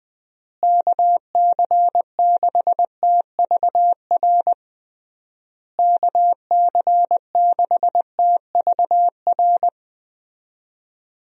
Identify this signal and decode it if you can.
Morse code, heard as a keyed tone, about 20 wpm, 700 Hz. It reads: KC6TVR KC6TVR
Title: USA Call Signs - 6 Characters - Repeated 3x - 20wpm